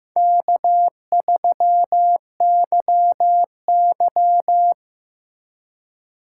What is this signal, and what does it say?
Morse code, heard as a keyed tone, about 18 wpm, 700 Hz. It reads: K3YY
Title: Speed Racing - USA Call Signs 15wpm